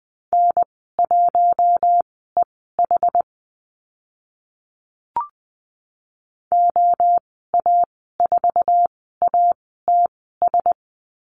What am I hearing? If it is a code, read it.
Morse code, heard as a keyed tone, about 19 wpm, spaced as for 14 wpm, 700 Hz. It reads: N1EH OA4ATS
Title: 076 Lesson 13g - Letter L - Call Signs 20wpm